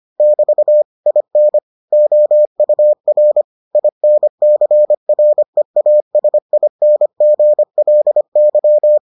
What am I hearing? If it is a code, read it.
Morse code, heard as a keyed tone, about 25 wpm, 600 Hz. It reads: <BT> IN OUR INCREASINGLY